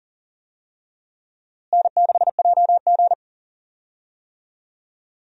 Morse code, heard as a keyed tone, about 40 wpm, 700 Hz. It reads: N6JG
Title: USA Call Signs - 4 Characters - Repeated 2x - 40wpm